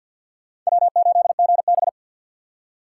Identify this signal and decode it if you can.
Morse code, heard as a keyed tone, about 61 wpm, 700 Hz. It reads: W8GB
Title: Speed Racing - USA 1x2 Call Signs 50wpm